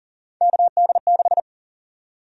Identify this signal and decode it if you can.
Morse code, heard as a keyed tone, about 40 wpm, 700 Hz. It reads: KD6
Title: USA Call Sign Prefixes - Repeated 2x - 40wpm